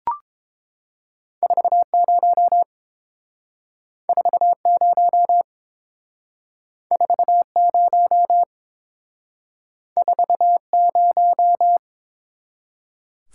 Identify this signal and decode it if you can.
Morse code, heard as a keyed tone, about 30 wpm, 700 Hz. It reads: 40 40 40 40